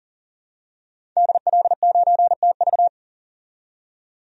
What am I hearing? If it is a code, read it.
Morse code, heard as a keyed tone, about 40 wpm, 700 Hz. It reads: DL9TV
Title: DXCC-100 Call Signs - 5 Characters - 40wpm